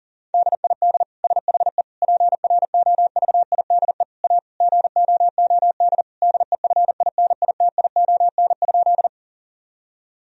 Morse code, heard as a keyed tone, about 38 wpm, 700 Hz. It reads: DID SHE PROVIDE A GOOD DEFINITION?
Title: Sentences from Top 1000 Words - Review Part 4 - 40wpm